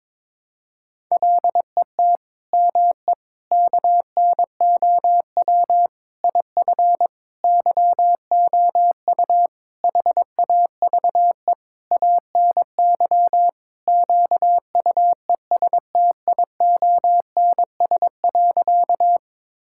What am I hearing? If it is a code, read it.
Morse code, heard as a keyed tone, about 22 wpm, 700 Hz. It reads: LET ME KNOW IF YOU HAVE ANY QUESTIONS.
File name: practice / 22wpm / Sentences with 8-Word Phrases - 22wpm